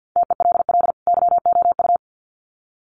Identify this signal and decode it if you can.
Morse code, heard as a keyed tone, about 50 wpm, 700 Hz. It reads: TELL YOU
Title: Sets of 2 Words 50wpm